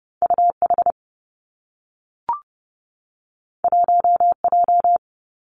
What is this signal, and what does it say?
Morse code, heard as a keyed tone, about 30 wpm, 700 Hz. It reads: U H 1 J
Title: Two Challenging Letter-Number Combinations 30wpm